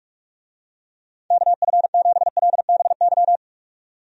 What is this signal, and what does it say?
Morse code, heard as a keyed tone, about 46 wpm, 700 Hz. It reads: KF7LBY
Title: USA Call Signs - 6 Characters - 45wpm